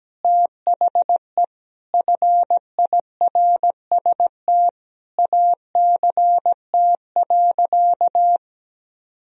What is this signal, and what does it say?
Morse code, heard as a keyed tone, about 17 wpm, 700 Hz. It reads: THE FIRST ACT.
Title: Sentences from Top 1000 Words - Review Part 2 - 17wpm